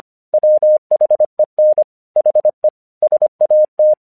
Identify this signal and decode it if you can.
Morse code, heard as a keyed tone, about 26 wpm, 610 Hz. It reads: WHEN HE SAT
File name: 1 English 25wpm / AQC36